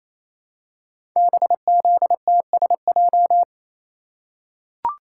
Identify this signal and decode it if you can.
Morse code, heard as a keyed tone, about 27 wpm, 700 Hz. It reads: BZTSJ
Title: Code Groups - 5 Letters - 28wpm